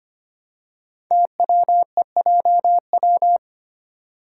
Morse code, heard as a keyed tone, about 25 wpm, 700 Hz. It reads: TWEJW
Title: Code Groups - 5 Letters - 25wpm